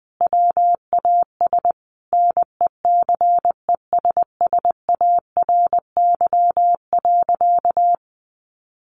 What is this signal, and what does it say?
Morse code, heard as a keyed tone, about 20 wpm, 700 Hz. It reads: WAS NECESSARY.
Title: Sentences from Top 1500 Words 20wpm